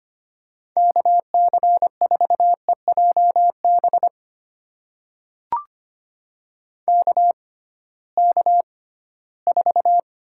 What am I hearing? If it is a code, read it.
Morse code, heard as a keyed tone, about 25 wpm, spaced as for 16 wpm, 700 Hz. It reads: KC4EJB KK4
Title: USA Call Signs - 6 Characters - 6x Character Spacing - 25wpm